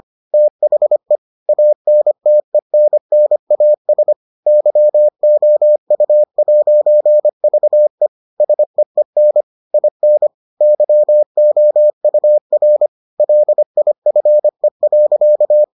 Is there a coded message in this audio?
Yes: Morse code from 0.3 to 15.6 s, heard as a keyed tone, about 25 wpm, 600 Hz. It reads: THE ANTENNAS YOU'VE SEEN IN YOUR LIFE.